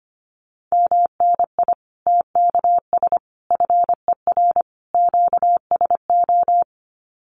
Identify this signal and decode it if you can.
Morse code, heard as a keyed tone, about 25 wpm, 700 Hz. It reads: MNI TKS FER QSO
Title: Common QSO Phrases 25wpm